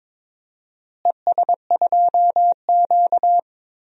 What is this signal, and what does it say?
Morse code, heard as a keyed tone, about 21 wpm, 700 Hz. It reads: ES2Q
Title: DXCC-100 Call Signs - 4 Characters - Repeated 2x - 22wpm